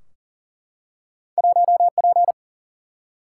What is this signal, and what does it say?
Morse code, heard as a keyed tone, about 40 wpm, 700 Hz. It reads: 1P
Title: Single-Letter Single-Number 40wpm